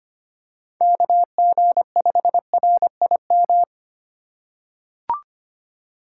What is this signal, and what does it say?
Morse code, heard as a keyed tone, about 25 wpm, 700 Hz. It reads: KG5RIM
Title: USA Call Signs - 3x Character Spacing - 25wpm